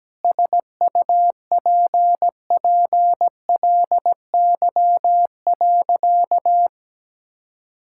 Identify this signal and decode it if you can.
Morse code, heard as a keyed tone, about 17 wpm, 700 Hz. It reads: SUPPLY.
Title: Sentences from Top 1000 Words - Review Part 5 - 17wpm